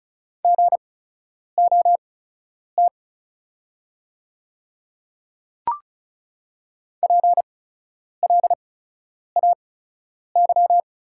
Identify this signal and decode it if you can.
Morse code, heard as a keyed tone, about 36 wpm, spaced as for 10 wpm, 700 Hz. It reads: GOT PLAY
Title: 8x Character Spacing - Top 300 Words - 35wpm